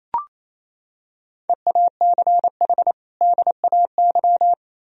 Morse code, heard as a keyed tone, about 28 wpm, 700 Hz. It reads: EACH DAY
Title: Sentences from Top 1000 Words - Review Part 1 - 28wpm